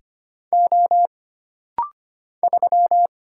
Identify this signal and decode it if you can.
Morse code, heard as a keyed tone, about 25 wpm, 700 Hz. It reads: O 3
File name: Single Character (Letter:Number:Punctuation:Prosign) - Rapid-Fire - 25wpm